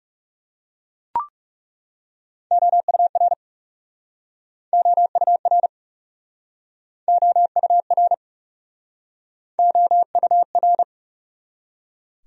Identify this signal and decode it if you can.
Morse code, heard as a keyed tone, about 40 wpm, 700 Hz. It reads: OUR OUR OUR OUR